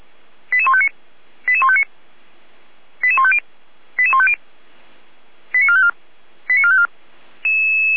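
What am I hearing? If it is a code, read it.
Five-tone selective call (ZVEI), heard as quick runs of five tones, tones of about 70 ms. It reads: 88159 88159 88150 88150 89454 89454
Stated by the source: Received on SDR as ZVEI five-tone selective call 88245.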